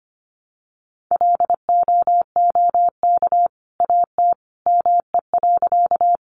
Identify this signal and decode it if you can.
Morse code, heard as a keyed tone, about 25 wpm, 700 Hz. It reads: LOOK AT ME.
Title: Sentences from Top 200 Words 25wpm